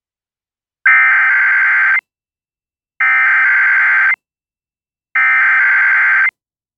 Emergency Alert System (SAME) tones (header bursts), heard as harsh buzzing bursts. It is heard from 0.9 to 6.3 s.